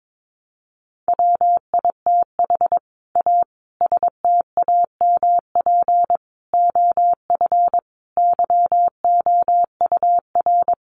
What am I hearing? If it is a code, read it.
Morse code, heard as a keyed tone, about 22 wpm, 700 Hz. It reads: WITH A STAMP OF YOUR
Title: A Sound Of Thunder 22wpm